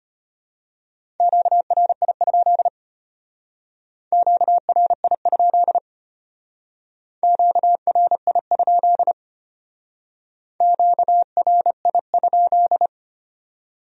Morse code, heard as a keyed tone, about 32 wpm, 700 Hz. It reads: QRI? QRI? QRI? QRI?